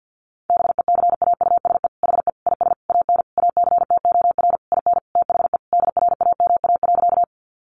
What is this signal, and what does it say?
Morse code, heard as a keyed tone, about 50 wpm, 700 Hz. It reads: BECAUSE HE IS AN ACTOR IN THE DRAMA.
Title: Sentences from Top 1500 Words 50wpm